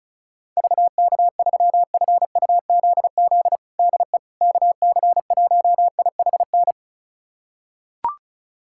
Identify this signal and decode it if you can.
Morse code, heard as a keyed tone, about 36 wpm, 700 Hz. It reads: VK3FUZZ DE KC1IHN
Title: USA Callsign DE Callsign 35wpm